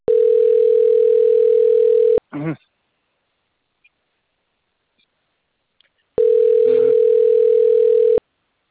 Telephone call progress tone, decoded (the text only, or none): ringback tone